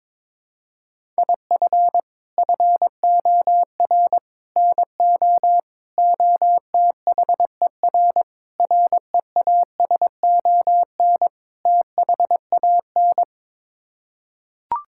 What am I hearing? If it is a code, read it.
Morse code, heard as a keyed tone, about 22 wpm, 700 Hz. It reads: IF FOR NO OTHER REASON THAN
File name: practice / 22wpm / Sets of 6 Words - 22wpm